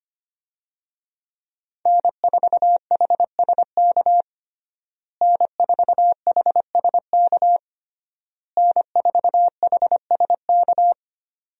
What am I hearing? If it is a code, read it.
Morse code, heard as a keyed tone, about 25 wpm, 700 Hz. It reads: N4HSK N4HSK N4HSK
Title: USA Call Signs - 5 Characters - Repeated 3x - 25wpm